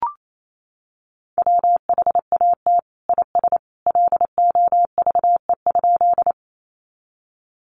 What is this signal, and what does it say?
Morse code, heard as a keyed tone, about 28 wpm, 700 Hz. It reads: WHAT IS LOVE?
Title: Sentences from Top 400 Words 28wpm